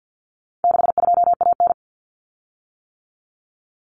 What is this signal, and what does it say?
Morse code, heard as a keyed tone, about 50 wpm, 700 Hz. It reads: 62AN
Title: Common Cut Numbers - 4-Digit 50wpm